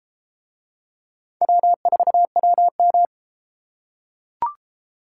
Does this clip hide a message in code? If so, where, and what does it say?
Yes: Morse code from 1.4 to 3.1 s, heard as a keyed tone, about 33 wpm, 700 Hz. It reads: W4WM